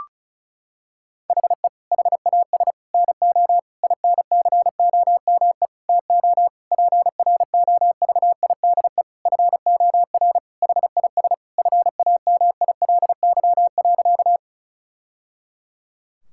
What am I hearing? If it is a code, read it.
Morse code, heard as a keyed tone, about 36 wpm, 700 Hz. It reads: HE HAS NO INCOME TO PROVIDE FOR HIS FAMILY.
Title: Sentences from Top 1000 Words - Review Part 5 - 35wpm